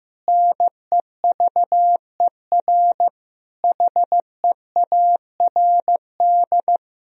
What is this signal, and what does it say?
Morse code, heard as a keyed tone, about 15 wpm, 700 Hz. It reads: NEVER HEARD
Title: Sentences from Top 1000 Words - Review Part 3 - 15wpm